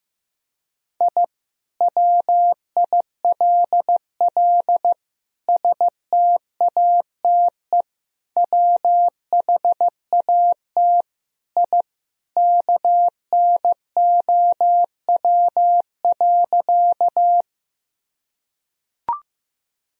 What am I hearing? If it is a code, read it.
Morse code, heard as a keyed tone, about 15 wpm, 700 Hz. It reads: I WILL STATE WHAT I KNOW.
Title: Sentences from Top 1000 Words - Review Part 2 - 15wpm